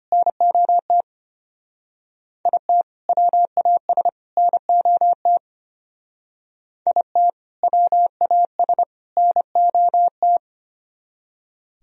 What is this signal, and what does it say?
Morse code, heard as a keyed tone, about 30 wpm, 700 Hz. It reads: NOT IT WAS NOT IT WAS NOT
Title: Speed-Racing - Sets of 3 Words 25wpm